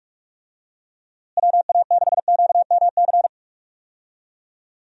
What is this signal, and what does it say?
Morse code, heard as a keyed tone, about 46 wpm, 700 Hz. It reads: WA6QMC
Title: USA Call Signs - 6 Characters - Repeated 2x - 45wpm